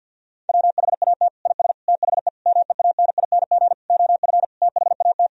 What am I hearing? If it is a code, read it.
Morse code, heard as a keyed tone, about 50 wpm, 700 Hz. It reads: WHAT IS THE MEANING OF THAT
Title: Sentences from Top 1000 Words - Review Part 4 - 50wpm